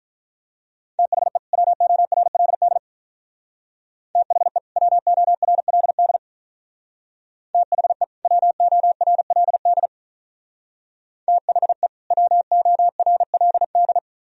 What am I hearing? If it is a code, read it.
Morse code, heard as a keyed tone, about 43 wpm, 700 Hz. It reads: THE WORLD THE WORLD THE WORLD THE WORLD
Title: Speed-Racing - Sets of 2 Words 35wpm